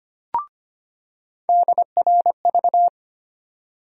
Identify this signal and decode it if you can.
Morse code, heard as a keyed tone, about 25 wpm, 700 Hz. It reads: DRV
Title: USA Call Sign Postfixes - 3 Letters 25wpm